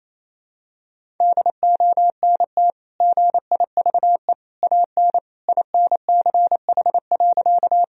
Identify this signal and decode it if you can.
Morse code, heard as a keyed tone, about 28 wpm, 700 Hz. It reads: DONT GIVE AN INCH.
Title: Sentences from Top 1000 Words - Review Part 3 - 28wpm